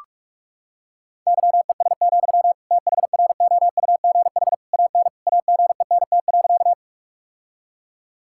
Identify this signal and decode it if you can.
Morse code, heard as a keyed tone, about 46 wpm, 700 Hz. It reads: YES, THROUGH AN AGENT.
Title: Sentences from Top 1600 Words 45wpm